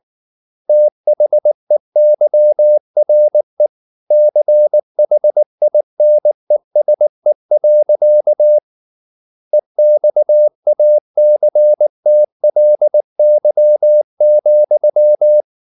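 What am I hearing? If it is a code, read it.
Morse code, heard as a keyed tone, about 19 wpm, 600 Hz. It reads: THEYRE CHINESE. EXACTLY,
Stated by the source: Morse code at 19 wpm from CW med Gnister, reading Ken Follett's novel Never in English.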